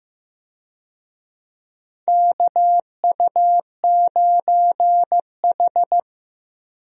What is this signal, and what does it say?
Morse code, heard as a keyed tone, about 16 wpm, 700 Hz. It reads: KU9H
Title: USA Call Signs - 4 Characters - Repeated 2x - 15wpm